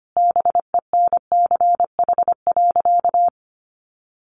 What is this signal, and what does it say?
Morse code, heard as a keyed tone, about 25 wpm, 700 Hz. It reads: BENCH.
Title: Sentences from Top 1400 Words 25wpm